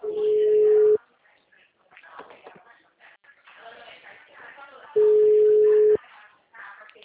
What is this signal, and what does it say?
Ringback tone (Europe, 425 Hz), with speech.
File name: out- -2010- - - .35336